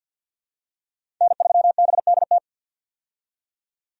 Morse code, heard as a keyed tone, about 46 wpm, 700 Hz. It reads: N3BDT
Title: USA Call Signs - 5 Characters - 50wpm